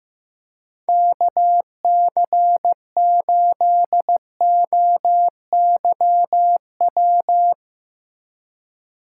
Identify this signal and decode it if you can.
Morse code, heard as a keyed tone, about 16 wpm, 700 Hz. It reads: KC8OYW
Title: Speed Racing - USA 2x3 Call Signs 15wpm